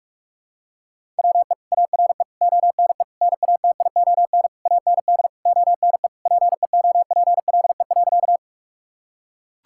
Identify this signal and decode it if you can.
Morse code, heard as a keyed tone, about 44 wpm, 700 Hz. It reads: WE ARE ONE NATION AND ONE PEOPLE.
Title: Sentences from Top 1000 Words - Review Part 6 - 45wpm